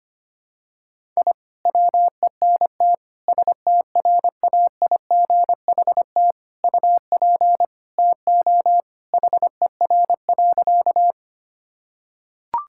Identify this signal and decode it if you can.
Morse code, heard as a keyed tone, about 25 wpm, 700 Hz. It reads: I WENT STRAIGHT UP TO HER.
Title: Sentences from Top 1000 Words - Review Part 6 - 25wpm